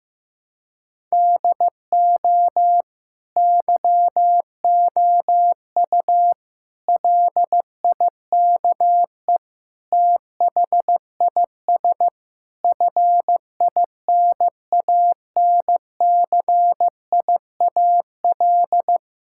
Morse code, heard as a keyed tone, about 15 wpm, 700 Hz. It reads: DO YOU LIKE THIS FINANCIAL
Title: Sentences from Top 1000 Words - Review Part 5 - 15wpm